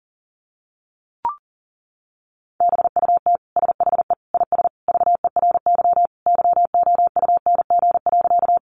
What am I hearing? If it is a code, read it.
Morse code, heard as a keyed tone, about 40 wpm, 700 Hz. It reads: BUT SHE IS VERY YOUNG.
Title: Sentences from Top 400 Words 40wpm